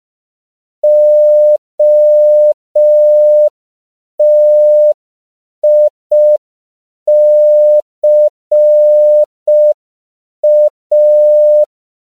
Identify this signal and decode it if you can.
Morse code, heard as a keyed tone, about 5 wpm, 600 Hz. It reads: OTICA
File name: CW / 5PPM